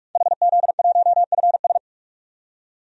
Morse code, heard as a keyed tone, about 46 wpm, 700 Hz. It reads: HZ1FS
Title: DXCC-100 Call Signs - 5 Characters - 45wpm